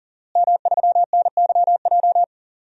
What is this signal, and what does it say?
Morse code, heard as a keyed tone, about 40 wpm, 700 Hz. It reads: M3NYJ
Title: DXCC-100 Call Signs - 5 Characters - Repeated 2x - 40wpm